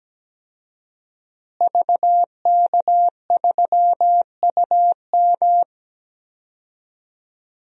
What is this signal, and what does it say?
Morse code, heard as a keyed tone, about 17 wpm, 700 Hz. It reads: VK3UM
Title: DXCC-100 Call Signs - 5 Characters - Repeated 3x - 17wpm